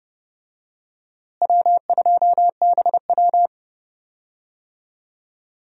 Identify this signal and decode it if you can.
Morse code, heard as a keyed tone, about 30 wpm, 700 Hz. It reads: W2BW
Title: USA Call Signs - 4 Characters - Repeated 3x - 30wpm